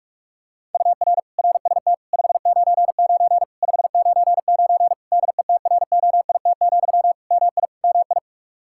Morse code, heard as a keyed tone, about 46 wpm, 700 Hz. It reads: UR RST 599 599 DETROIT, MI MI
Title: Practice QSO - Exchange 1 - Beginning - 45wpm